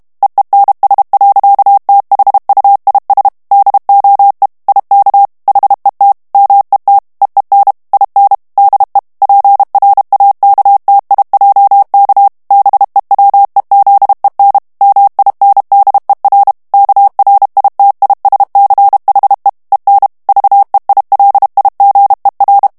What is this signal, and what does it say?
Morse code, heard as a keyed tone, about 32 wpm, 800 Hz. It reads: EENS. THUIS DOE IK HET MET EEN IN DE PRAKTIJK BEWEZEN MINDER KRITISCHE EN VEILIGER
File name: HOLLANDAIS texte clair 2 vitesse 30 mots minute